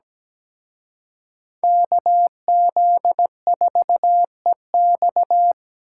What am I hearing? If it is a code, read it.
Morse code, heard as a keyed tone, about 20 wpm, 700 Hz. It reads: KZ4EX
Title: Speed Racing - USA 2x2 Call Signs 17wpm